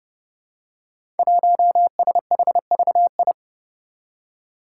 Morse code, heard as a keyed tone, about 30 wpm, 700 Hz. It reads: 1SHVI
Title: Five Challenging Letter-Number Combinations 30wpm